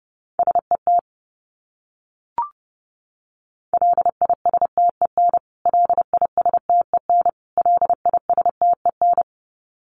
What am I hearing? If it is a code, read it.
Morse code, heard as a keyed tone, about 30 wpm, 700 Hz. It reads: SET LISTEN LISTEN LISTEN
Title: Top 500 Words Repeated 30wpm